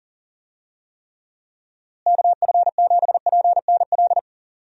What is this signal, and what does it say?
Morse code, heard as a keyed tone, about 40 wpm, 700 Hz. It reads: KF7PNL
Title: USA Call Signs - 6 Characters - Repeated 2x - 40wpm